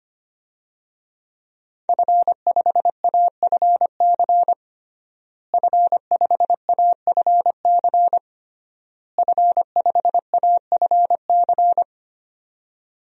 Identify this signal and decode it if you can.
Morse code, heard as a keyed tone, about 25 wpm, 700 Hz. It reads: F5AFC F5AFC F5AFC
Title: DXCC-100 Call Signs - 5 Characters - Repeated 3x - 25wpm